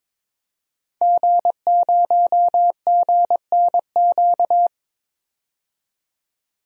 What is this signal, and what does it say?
Morse code, heard as a keyed tone, about 22 wpm, 700 Hz. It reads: G0GNQ
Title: DXCC-100 Call Signs - 5 Characters - 22wpm